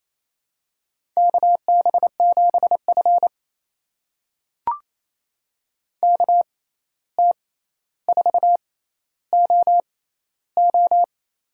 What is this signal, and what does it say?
Morse code, heard as a keyed tone, about 28 wpm, spaced as for 13 wpm, 700 Hz. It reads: KB7F KT4OO
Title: USA Call Signs - 6x Character Spacing - 28wpm